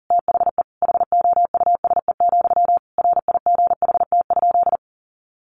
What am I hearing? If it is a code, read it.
Morse code, heard as a keyed tone, about 40 wpm, 700 Hz. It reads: THE HOUSE, RIGHT?
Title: Sentences from Top 1200 Words 40wpm